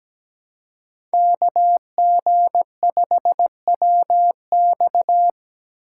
Morse code, heard as a keyed tone, about 20 wpm, 700 Hz. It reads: KG5WX
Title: Speed Racing - USA 2x2 Call Signs 17wpm